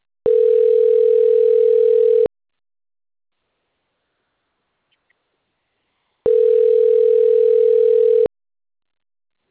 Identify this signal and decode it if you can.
Ringback tone (North America), 440 + 480 Hz.